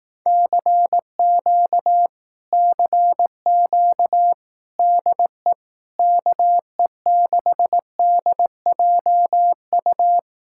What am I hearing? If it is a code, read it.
Morse code, heard as a keyed tone, about 18 wpm, 700 Hz. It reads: CQ CQ DE KE6DJU
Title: Speed-Racing Common QSO Phrases Encore - 15wpm